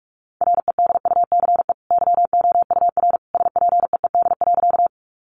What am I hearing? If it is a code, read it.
Morse code, heard as a keyed tone, about 44 wpm, 700 Hz. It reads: REDUCE YOUR SPEED.
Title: Sentences from Top 1100 Words 45wpm